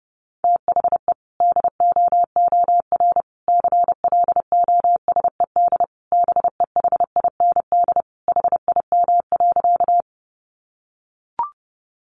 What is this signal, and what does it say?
Morse code, heard as a keyed tone, about 30 wpm, 700 Hz. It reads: THE DOOR CLOSED BEHIND HIM.
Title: Sentences from Top 1400 Words 30wpm